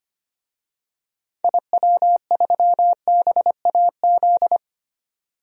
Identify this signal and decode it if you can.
Morse code, heard as a keyed tone, about 25 wpm, 700 Hz. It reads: IW3BAZ
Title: DXCC-100 Call Signs - 6 Characters - 25wpm